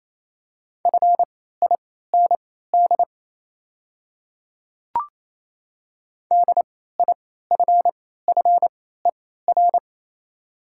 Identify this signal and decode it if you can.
Morse code, heard as a keyed tone, about 27 wpm, spaced as for 16 wpm, 690 Hz. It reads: FIND DIFFER
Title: 3x Character Spacing - Top 200 Words - 28wpm